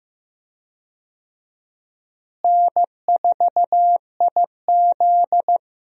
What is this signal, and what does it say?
Morse code, heard as a keyed tone, about 15 wpm, 700 Hz. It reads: N4IZ
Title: USA Call Signs - 4 Characters - Repeated 2x - 15wpm